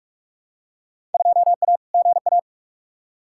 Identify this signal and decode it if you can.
Morse code, heard as a keyed tone, about 47 wpm, 700 Hz. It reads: 2A GA
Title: Field Day Exchanges - 45wpm